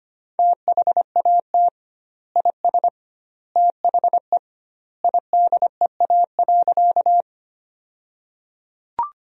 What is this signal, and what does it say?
Morse code, heard as a keyed tone, about 25 wpm, 700 Hz. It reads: THAT IS THE IDEA.
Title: 2x Word-Spacing - Sentences from Top 300 Words - 25wpm